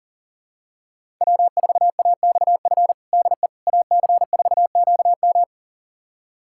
Morse code, heard as a keyed tone, about 40 wpm, 700 Hz. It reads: W4AXF DE AC4QM
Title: USA Callsign DE Callsign - 5 Characters - 40wpm